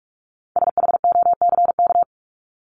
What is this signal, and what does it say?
Morse code, heard as a keyed tone, about 44 wpm, 700 Hz. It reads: SHOCK
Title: Top 1751-2000 Words 45wpm